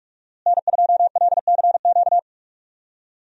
Morse code, heard as a keyed tone, about 46 wpm, 700 Hz. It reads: N2LCQ
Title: USA Call Signs - 5 Characters - Repeated 2x - 45wpm